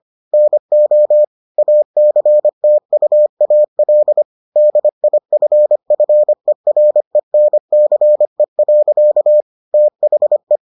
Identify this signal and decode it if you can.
Morse code, heard as a keyed tone, about 25 wpm, 600 Hz. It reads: NO ACTUAL DIFFERENCE. THE